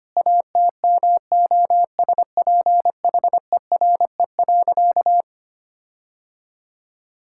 Morse code, heard as a keyed tone, about 25 wpm, 700 Hz. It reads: ATMOSPHERE.
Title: Sentences from Top 1600 Words 25wpm